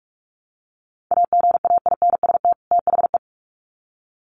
Morse code, heard as a keyed tone, about 44 wpm, 700 Hz. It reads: AGAINST THE
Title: Sets of 2 Words 45wpm